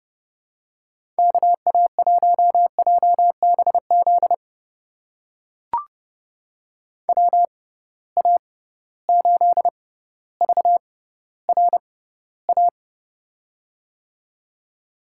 Morse code, heard as a keyed tone, about 32 wpm, spaced as for 16 wpm, 700 Hz. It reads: KA1JBZ WA8VRA
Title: USA Call Signs - 6 Characters - 6x Character Spacing - 30wpm